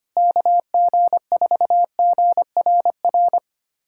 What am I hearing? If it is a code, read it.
Morse code, heard as a keyed tone, about 25 wpm, 700 Hz. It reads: KG4GRR
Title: USA Callsign DE Callsign - 6 Characters - 25wpm